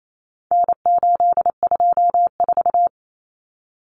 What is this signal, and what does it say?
Morse code, heard as a keyed tone, about 28 wpm, 700 Hz. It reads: N824
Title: Common Cut Numbers - 4-Digit 28wpm